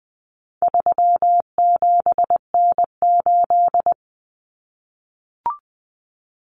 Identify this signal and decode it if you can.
Morse code, heard as a keyed tone, about 20 wpm, 700 Hz. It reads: 37N8
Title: Common Cut Numbers - 4-Digit 20wpm